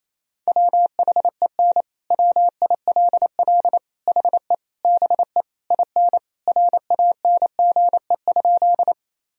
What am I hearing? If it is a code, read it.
Morse code, heard as a keyed tone, about 28 wpm, 700 Hz. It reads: WHEN WILL HE BE IN RANGE?
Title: Sentences from Top 1000 Words - Review Part 4 - 28wpm